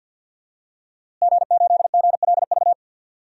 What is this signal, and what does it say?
Morse code, heard as a keyed tone, about 46 wpm, 700 Hz. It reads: G8GLV
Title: DXCC-100 Call Signs - 5 Characters - Repeated 3x - 50wpm